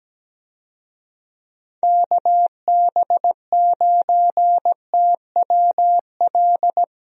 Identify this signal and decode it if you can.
Morse code, heard as a keyed tone, about 17 wpm, 700 Hz. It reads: KB9TWL
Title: USA Call Signs - 6 Characters - Repeated 2x - 17wpm